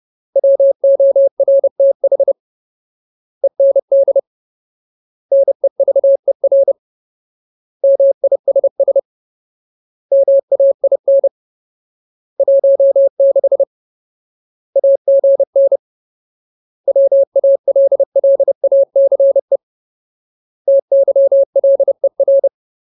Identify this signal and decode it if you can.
Morse code, heard as a keyed tone, about 30 wpm, 550 Hz. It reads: WORTH END NEVER MISS MAIN 16 AGN WALLACE TYLER
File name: Ultimate Word List / 30WPM Ultimate